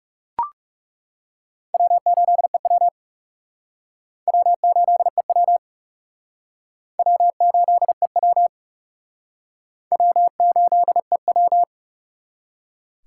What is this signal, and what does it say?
Morse code, heard as a keyed tone, about 38 wpm, 700 Hz. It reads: W8EW W8EW W8EW W8EW
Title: Speed Racing - USA 1x2 Call Signs 30wpm